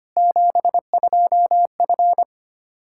Morse code, heard as a keyed tone, about 25 wpm, 700 Hz. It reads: 72F
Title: K1USN SST QSO Practice 25wpm